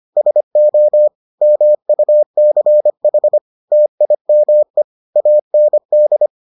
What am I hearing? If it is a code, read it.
Morse code, heard as a keyed tone, about 25 wpm, 600 Hz. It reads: SO MUCH TIME AND